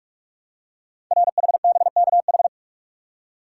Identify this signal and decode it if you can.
Morse code, heard as a keyed tone, about 44 wpm, 700 Hz. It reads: RHBKH
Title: Code Groups - 5 Letters - 45wpm